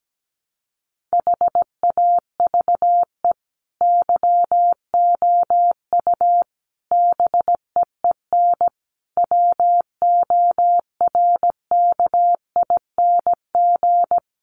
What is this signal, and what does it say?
Morse code, heard as a keyed tone, about 17 wpm, 700 Hz. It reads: HAVE YOU BEEN WORKING
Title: Sentences from Top 700 Words 17wpm